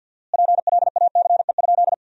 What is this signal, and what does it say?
Morse code, heard as a keyed tone, about 50 wpm, 700 Hz. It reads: PLACE?
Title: 4x Word-Spacing - Sentences from Top 200 Words - 50wpm